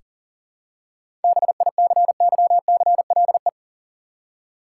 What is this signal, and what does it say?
Morse code, heard as a keyed tone, about 40 wpm, 700 Hz. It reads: BICYCLE
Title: Seven Letter Words - 40wpm